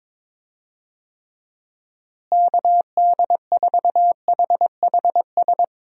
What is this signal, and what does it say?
Morse code, heard as a keyed tone, about 22 wpm, 700 Hz. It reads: KD4HHS
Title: USA Call Signs - 6 Characters - Repeated 2x - 22wpm